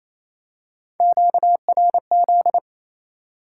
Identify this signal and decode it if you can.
Morse code, heard as a keyed tone, about 28 wpm, 700 Hz. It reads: QRZ
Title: Top 90 QSO Elements 28wpm